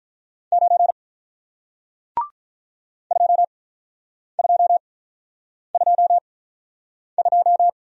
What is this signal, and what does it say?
Morse code, heard as a keyed tone, about 47 wpm, spaced as for 9 wpm, 700 Hz. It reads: 9 2222